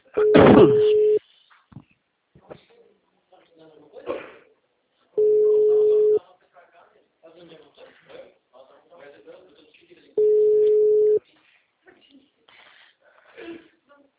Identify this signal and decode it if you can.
Ringback tone (Europe, 425 Hz), with speech.
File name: out- -2011- - - .26